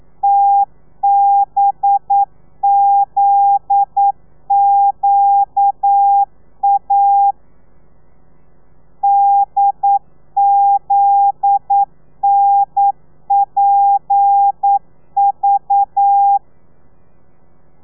Morse code, heard as a keyed tone, about 9 wpm, 800 Hz. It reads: TBZQA DZNPV
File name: telegrafia / i2tfj / Lez9 40